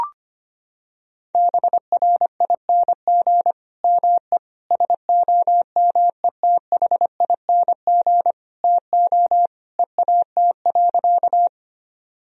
Morse code, heard as a keyed tone, about 25 wpm, 700 Hz. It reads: BRING ME SOMETHING TO EAT.
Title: Sentences from Top 1000 Words - Review Part 3 - 25wpm